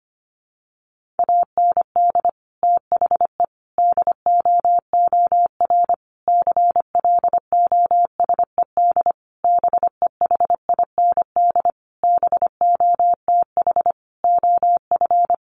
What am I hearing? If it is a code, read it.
Morse code, heard as a keyed tone, about 25 wpm, 700 Hz. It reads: AND THE DOOR CLOSED BEHIND BOTH OF
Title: Sentences from Top 1400 Words 25wpm